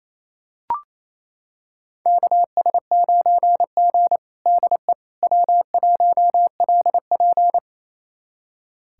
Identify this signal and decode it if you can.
Morse code, heard as a keyed tone, about 28 wpm, 700 Hz. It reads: KS9G DE W1LP
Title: USA Callsign DE Callsign - 4 Characters - 28wpm